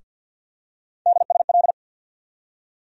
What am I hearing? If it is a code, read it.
Morse code, heard as a keyed tone, about 47 wpm, 700 Hz. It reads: DSL
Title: USA Call Sign Postfixes - 3 Letters 50wpm